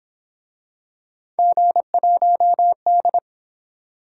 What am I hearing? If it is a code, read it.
Morse code, heard as a keyed tone, about 29 wpm, 700 Hz. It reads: G1D